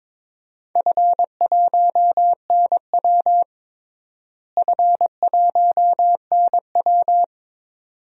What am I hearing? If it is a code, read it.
Morse code, heard as a keyed tone, about 22 wpm, 700 Hz. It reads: F1NW F1NW
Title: DXCC-100 Call Signs - 4 Characters - Repeated 2x - 22wpm